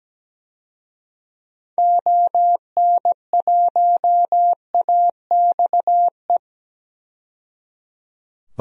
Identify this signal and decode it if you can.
Morse code, heard as a keyed tone, about 17 wpm, 700 Hz. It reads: ON1AXE
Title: DXCC-100 Call Signs - 6 Characters - 17wpm